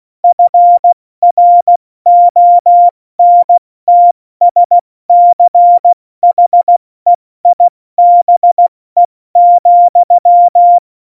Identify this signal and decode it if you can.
Morse code, heard as a keyed tone, about 16 wpm, 700 Hz. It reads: FRONTSCHEIBE,
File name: Morsekurs / Witze / Witz 05. Januar 2021, 70 F80